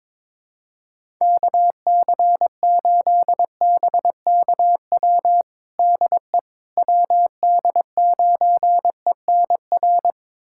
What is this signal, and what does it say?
Morse code, heard as a keyed tone, about 21 wpm, 700 Hz. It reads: KC8BKW DE WD9ENR
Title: USA Callsign DE Callsign - 6 Characters - 22wpm